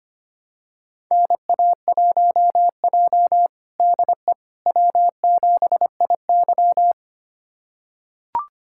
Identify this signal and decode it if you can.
Morse code, heard as a keyed tone, about 25 wpm, 700 Hz. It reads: NA1J DE W7IY
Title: USA Callsign DE Callsign - 4 Characters - 25wpm